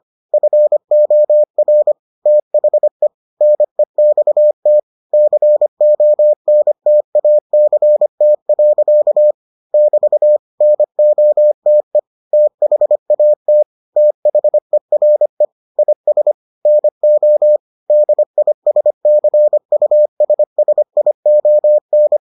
Morse code, heard as a keyed tone, about 25 wpm, 600 Hz. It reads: FOR THE NEXT CONTACT. <BT> NOTE THAT THERE IS NO DISCUSSION